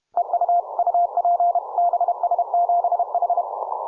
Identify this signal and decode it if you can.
Morse code, heard as a keyed tone, about 24 wpm, 700 Hz. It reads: EU UP BS7H